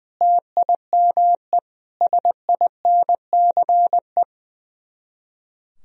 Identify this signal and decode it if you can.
Morse code, heard as a keyed tone, about 20 wpm, 700 Hz. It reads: TIME SINCE
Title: Speed-Racing - Sets of 4 Words 20wpm